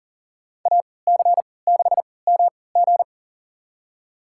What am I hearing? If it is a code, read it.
Morse code, heard as a keyed tone, about 40 wpm, spaced as for 23 wpm, 700 Hz. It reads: AC6MG
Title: USA Call Signs - 5 Characters - 3x Character Spacing - 40wpm